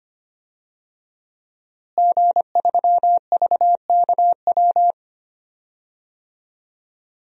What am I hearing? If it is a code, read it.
Morse code, heard as a keyed tone, about 25 wpm, 700 Hz. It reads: G3VKW
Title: DXCC-100 Call Signs - 5 Characters - Repeated 2x - 25wpm